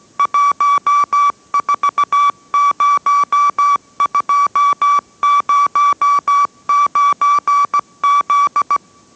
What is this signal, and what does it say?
Morse code, heard as a keyed tone, about 17 wpm, 1150 Hz. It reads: 140209Z